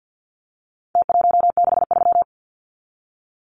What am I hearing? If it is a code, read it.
Morse code, heard as a keyed tone, about 48 wpm, 700 Hz. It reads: T163
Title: Numbers - Contest Style - 4-Digit 50wpm